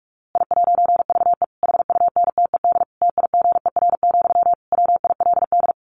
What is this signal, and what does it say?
Morse code, heard as a keyed tone, about 44 wpm, 700 Hz. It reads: I'VE HUNTED TIGER, WILD
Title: A Sound Of Thunder 45wpm